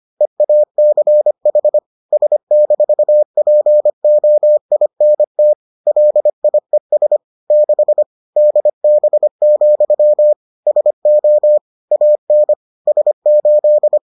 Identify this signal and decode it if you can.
Morse code, heard as a keyed tone, about 25 wpm, 600 Hz. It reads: EACH S-POINT LIES 6 DB, SO AN S8